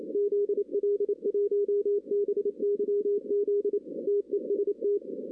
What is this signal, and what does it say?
Morse code, heard as a keyed tone, about 28 wpm, 410 Hz. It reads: ZL1BYZ TEST
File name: 2016 arrl dx cw / 2016-ARRL-DX-CW-K5TR- - 0-9